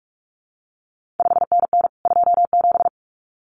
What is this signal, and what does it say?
Morse code, heard as a keyed tone, about 46 wpm, 700 Hz. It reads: 5NN 27
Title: CQ WW DX Contest Exchange Encore - 45wpm